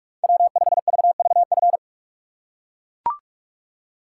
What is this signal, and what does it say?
Morse code, heard as a keyed tone, about 44 wpm, 700 Hz. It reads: W5VVF